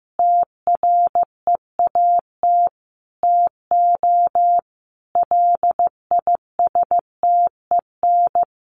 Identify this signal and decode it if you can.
Morse code, heard as a keyed tone, about 15 wpm, 700 Hz. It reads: TREAT TO LISTEN
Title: Sentences from Top 1300 Words 15wpm